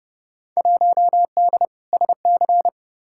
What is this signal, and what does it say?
Morse code, heard as a keyed tone, about 30 wpm, 700 Hz. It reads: 1D SC
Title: Field Day Exchanges - 30wpm